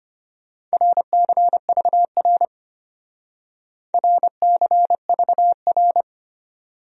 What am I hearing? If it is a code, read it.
Morse code, heard as a keyed tone, about 32 wpm, 700 Hz. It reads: RCVR RCVR